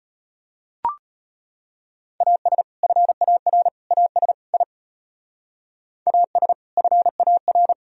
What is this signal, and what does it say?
Morse code, heard as a keyed tone, about 30 wpm, 700 Hz. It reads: AS FAR AS I AS FAR
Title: Speed-Racing - Sets of 4 Words 25wpm